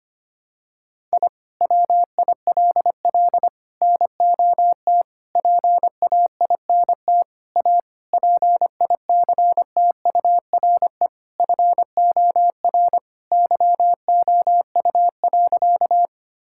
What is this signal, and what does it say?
Morse code, heard as a keyed tone, about 25 wpm, 700 Hz. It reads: I WILL NOT PAINT A PICTURE FOR YOU.
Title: Sentences from Top 1000 Words - Review Part 6 - 25wpm